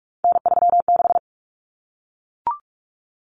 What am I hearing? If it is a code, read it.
Morse code, heard as a keyed tone, about 44 wpm, 700 Hz. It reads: N36
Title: Common Cut Numbers - 3-Digit 45wpm